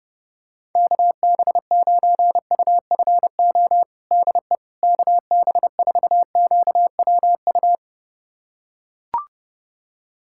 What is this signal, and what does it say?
Morse code, heard as a keyed tone, about 30 wpm, 700 Hz. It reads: KB9UFO DE KB4QWU
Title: USA Callsign DE Callsign - 6 Characters - 30wpm